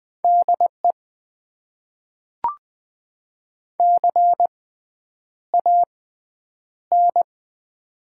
Morse code, heard as a keyed tone, about 20 wpm, spaced as for 9 wpm, 700 Hz. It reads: DE CAN
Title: Top 90 QSO Elements - 6x Character Spacing - 20wpm